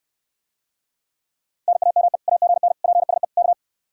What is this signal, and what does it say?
Morse code, heard as a keyed tone, about 69 wpm, 700 Hz. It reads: NAME AGN PSE K